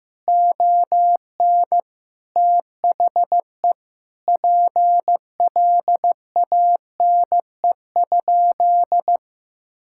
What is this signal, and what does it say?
Morse code, heard as a keyed tone, about 15 wpm, 700 Hz. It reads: ON THE PLANE?
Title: Sentences from Top 1000 Words - Review Part 6 - 15wpm